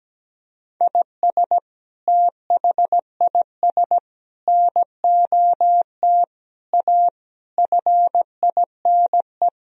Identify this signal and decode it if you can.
Morse code, heard as a keyed tone, about 17 wpm, 700 Hz. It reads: IS THIS NOT A FINE
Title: Sentences from Top 1000 Words - Review Part 2 - 17wpm